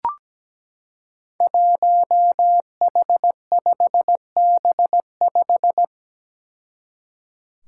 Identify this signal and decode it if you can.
Morse code, heard as a keyed tone, about 17 wpm, 700 Hz. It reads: 1H5B5
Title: Five Challenging Letter-Number Combinations 17wpm